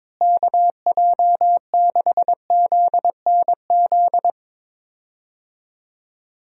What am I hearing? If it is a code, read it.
Morse code, heard as a keyed tone, about 22 wpm, 700 Hz. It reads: KJ6ZNZ
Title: USA Call Signs - 6 Characters - Repeated 3x - 22wpm